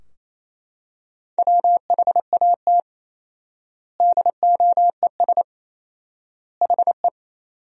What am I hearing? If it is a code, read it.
Morse code, heard as a keyed tone, about 28 wpm, 700 Hz. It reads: WHAT DOES HE
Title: 4x Word-Spacing - Sentences from Top 200 Words - 28wpm